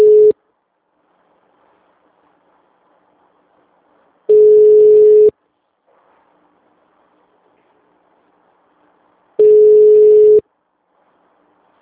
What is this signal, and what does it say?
Ringback tone (Europe, 425 Hz).